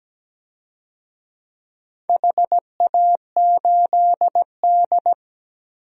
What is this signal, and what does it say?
Morse code, heard as a keyed tone, about 17 wpm, 700 Hz. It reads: HA8D
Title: DXCC-100 Call Signs - 4 Characters - Repeated 3x - 17wpm